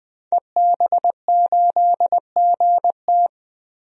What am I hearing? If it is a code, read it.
Morse code, heard as a keyed tone, about 21 wpm, 700 Hz. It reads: EB8GT
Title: DXCC-100 Call Signs - 5 Characters - 20wpm